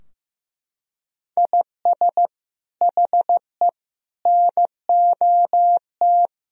Morse code, heard as a keyed tone, about 16 wpm, 700 Hz. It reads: IS HE NOT
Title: Sentences from Top 1000 Words - Review Part 2 - 15wpm